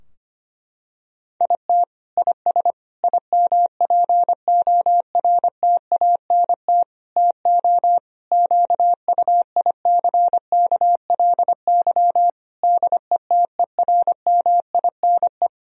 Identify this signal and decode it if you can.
Morse code, heard as a keyed tone, about 25 wpm, 700 Hz. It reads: IT IS IMPORTANT TO QUICKLY DETERMINE
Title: Sentences from Top 1000 Words - Review Part 4 - 25wpm